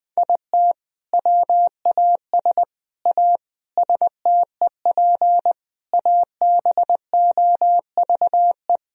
Morse code, heard as a keyed tone, about 21 wpm, 700 Hz. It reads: IT WAS A STEP ABOVE
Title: Sentences from Top 1000 Words - Review Part 3 - 20wpm